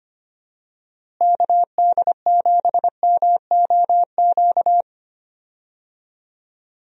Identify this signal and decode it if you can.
Morse code, heard as a keyed tone, about 25 wpm, 700 Hz. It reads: KD7MOQ
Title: USA Call Signs - 6 Characters - Repeated 3x - 25wpm